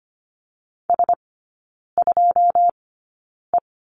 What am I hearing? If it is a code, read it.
Morse code, heard as a keyed tone, about 25 wpm, spaced as for 9 wpm, 690 Hz. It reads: S2E
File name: Lesson 14e - Number 2 - Three Characters 25wpm